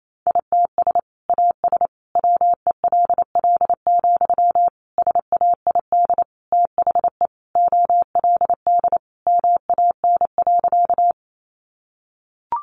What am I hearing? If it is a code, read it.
Morse code, heard as a keyed tone, about 28 wpm, 700 Hz. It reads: ITS AS WELL, SAID THE OLD MAN.
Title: Sentences from Top 600 Words 28wpm